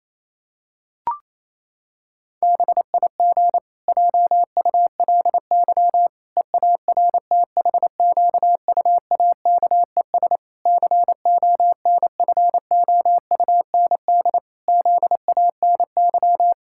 Morse code, heard as a keyed tone, about 28 wpm, 700 Hz. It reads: BIG JULY EARTHQUAKES CONFOUND ZANY
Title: Pangram Challenge v2 - 28wpm